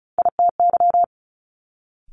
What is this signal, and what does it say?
Morse code, heard as a keyed tone, about 33 wpm, 700 Hz. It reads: ITY
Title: Multi-letter Phonemes Quick Fire 35wpm